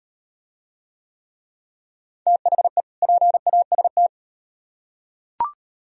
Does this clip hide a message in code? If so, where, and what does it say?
Yes: Morse code from 2.3 to 4.1 s, heard as a keyed tone, about 38 wpm, 700 Hz. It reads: THE PAST